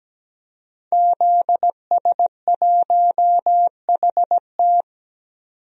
Morse code, heard as a keyed tone, about 17 wpm, 700 Hz. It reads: ZS1HT
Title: DXCC-100 Call Signs - 5 Characters - 17wpm